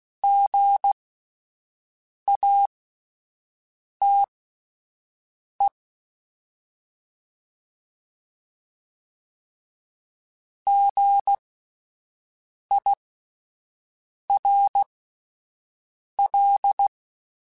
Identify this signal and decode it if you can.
Morse code, heard as a keyed tone, about 16 wpm, spaced as for 5 wpm, 800 Hz. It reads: GATE GIRL